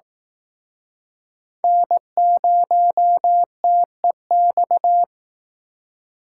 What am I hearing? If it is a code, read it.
Morse code, heard as a keyed tone, about 19 wpm, 700 Hz. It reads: N0TEX
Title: Speed Racing - USA 1x3 Call Signs 15wpm